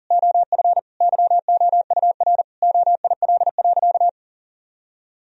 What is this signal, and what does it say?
Morse code, heard as a keyed tone, about 40 wpm, 700 Hz. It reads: OF YOUR OIL.
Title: Sentences from Top 1000 Words - Review Part 4 - 40wpm